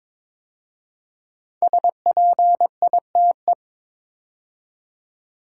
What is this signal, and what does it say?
Morse code, heard as a keyed tone, about 22 wpm, 700 Hz. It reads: SPITE
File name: 5 Letter Words - First 3 Letters Given - 22wpm